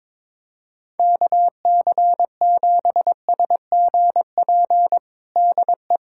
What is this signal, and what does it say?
Morse code, heard as a keyed tone, about 22 wpm, 700 Hz. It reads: KC7SGP DE
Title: USA Callsign DE Callsign 22wpm